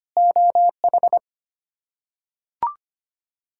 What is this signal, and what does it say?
Morse code, heard as a keyed tone, about 25 wpm, 700 Hz. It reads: OH
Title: K1USN SST QSO Practice - 25wpm